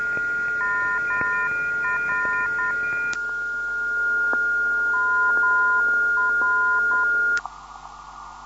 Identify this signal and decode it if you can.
Morse code, heard as a keyed tone, about 10 wpm, 1020 Hz, with noise. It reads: MR MR
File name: ndb zvok / mr 334khz